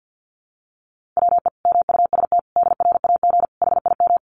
Morse code, heard as a keyed tone, about 50 wpm, 700 Hz. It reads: WE MUST DRAG HIM
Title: Sentences from Top 1500 Words 50wpm